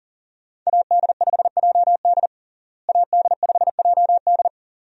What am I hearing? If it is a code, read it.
Morse code, heard as a keyed tone, about 37 wpm, 700 Hz. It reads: AD5JD AD5JD
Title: USA Call Signs - 5 Characters - Repeated 2x - 40wpm